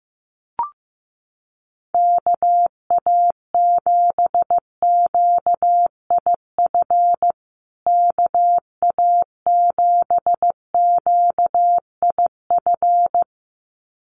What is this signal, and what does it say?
Morse code, heard as a keyed tone, about 15 wpm, 700 Hz. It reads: KA7QIF KA7QIF
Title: USA Call Signs Repeated 2x 15wpm